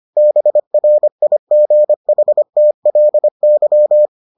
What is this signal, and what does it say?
Morse code, heard as a keyed tone, about 25 wpm, 600 Hz. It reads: BRIGHTLY